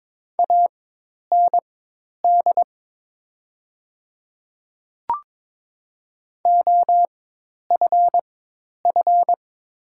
Morse code, heard as a keyed tone, about 22 wpm, spaced as for 10 wpm, 700 Hz. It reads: AND OFF
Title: 4x Character Spacing - Top 200 Words - 22wpm